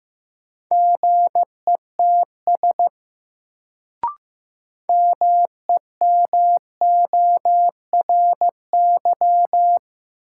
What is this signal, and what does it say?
Morse code, heard as a keyed tone, about 15 wpm, 700 Hz. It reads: GETS MEMORY
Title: Sentences from Top 1300 Words - New Word Drill Alt - 15wpm